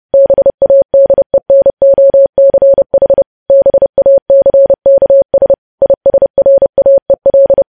Morse code, heard as a keyed tone, about 30 wpm, 550 Hz. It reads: BADENOCH BACKS ISRAEL
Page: news headlines in morse code at 30 wpm